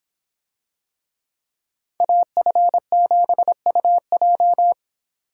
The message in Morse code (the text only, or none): AF7UJ